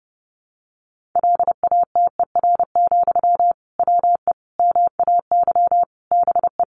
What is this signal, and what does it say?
Morse code, heard as a keyed tone, about 28 wpm, 700 Hz. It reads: LATER, WE MAY BE
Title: Sentences from Top 800 Words 30wpm